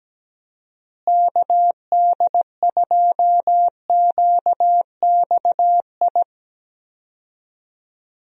Morse code, heard as a keyed tone, about 17 wpm, 700 Hz. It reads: KD2QXI
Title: USA Call Signs - 6 Characters - Repeated 3x - 17wpm